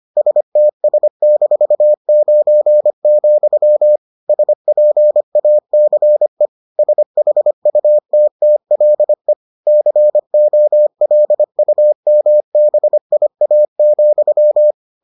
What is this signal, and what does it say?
Morse code, heard as a keyed tone, about 25 wpm, 600 Hz. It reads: STS-9, SPACE SHUTTLE COLUMBIA,